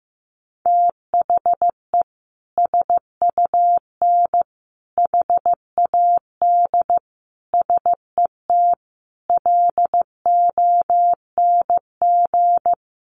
Morse code, heard as a keyed tone, about 15 wpm, 700 Hz. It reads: THE SUN HAD SET LONG
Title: Sentences from Top 300 Words 15wpm